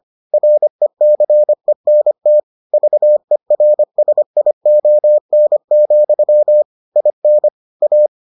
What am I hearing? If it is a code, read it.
Morse code, heard as a keyed tone, about 25 wpm, 600 Hz. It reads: RECENT VERSION, IN A